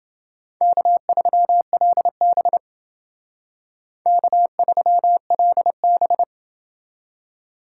Morse code, heard as a keyed tone, about 28 wpm, 700 Hz. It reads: K3LB K3LB